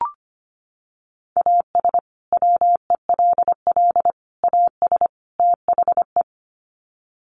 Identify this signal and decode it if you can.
Morse code, heard as a keyed tone, about 25 wpm, 700 Hz. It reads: AS WELL AS THE
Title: Sets of 4 Words 25wpm